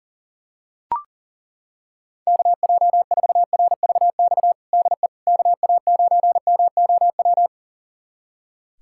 Morse code, heard as a keyed tone, about 40 wpm, 700 Hz. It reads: KJ4RVX DE KA9MOW
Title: USA Callsign DE Callsign - 6 Characters - 40wpm